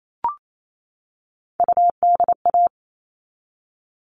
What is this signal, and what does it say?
Morse code, heard as a keyed tone, about 28 wpm, 700 Hz. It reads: UDA
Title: All Cut Numbers - 3-Digit 28wpm